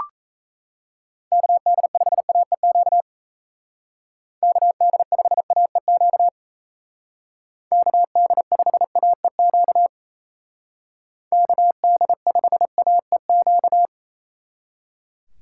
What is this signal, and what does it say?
Morse code, heard as a keyed tone, about 36 wpm, 700 Hz. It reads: KD5AEQ KD5AEQ KD5AEQ KD5AEQ